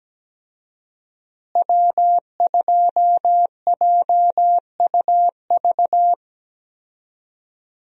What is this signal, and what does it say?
Morse code, heard as a keyed tone, about 17 wpm, 700 Hz. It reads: W2JUV
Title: USA Call Signs - 5 Characters - Repeated 2x - 17wpm